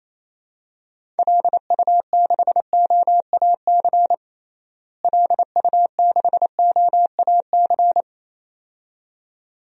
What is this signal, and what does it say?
Morse code, heard as a keyed tone, about 28 wpm, 700 Hz. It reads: LU6OAC LU6OAC
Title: DXCC-100 Call Signs - 6 Characters - Repeated 2x - 28wpm